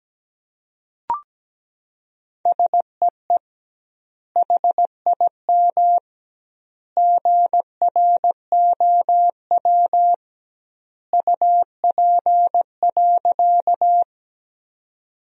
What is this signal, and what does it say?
Morse code, heard as a keyed tone, about 17 wpm, 700 Hz. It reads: SEE HIM GROW UP.
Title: 2x Word-Spacing - Sentences from Top 300 Words - 17wpm